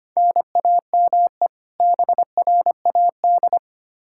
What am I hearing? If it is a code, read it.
Morse code, heard as a keyed tone, about 25 wpm, 700 Hz. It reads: NAME BRAD
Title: Practice QSO - Exchange 1 - Beginning - 25wpm